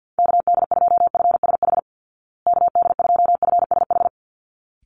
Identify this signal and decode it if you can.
Morse code, heard as a keyed tone, about 50 wpm, 700 Hz. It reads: KD2FSH KD2FSH
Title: Call Signs Repeated 3x - 50wpm